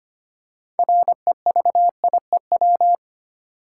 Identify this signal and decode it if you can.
Morse code, heard as a keyed tone, about 25 wpm, 700 Hz. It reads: REVIEW
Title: Six Letter Words - 25wpm